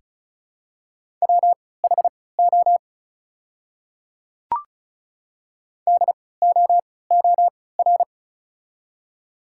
Morse code, heard as a keyed tone, about 34 wpm, spaced as for 20 wpm, 700 Hz. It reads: WHO DOOR
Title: 3x Character Spacing - Top 300 Words - 35wpm